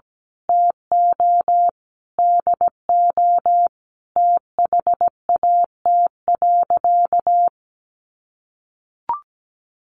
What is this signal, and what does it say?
Morse code, heard as a keyed tone, about 17 wpm, 700 Hz. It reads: TO DO THAT.
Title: Sentences from Top 1500 Words 17wpm